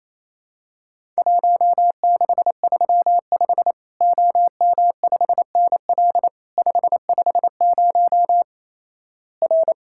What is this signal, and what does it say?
Morse code, heard as a keyed tone, about 28 wpm, 700 Hz. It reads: 1635 OM5NL 550 R
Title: Worked All Europe DX Contest - QTC Exchange - 28wpm